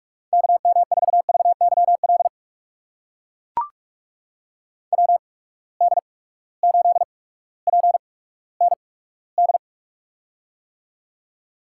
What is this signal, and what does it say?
Morse code, heard as a keyed tone, about 43 wpm, spaced as for 19 wpm, 700 Hz. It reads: KM4VYL WD8PND
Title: USA Call Signs - 6 Characters - 8x Character Spacing - 45wpm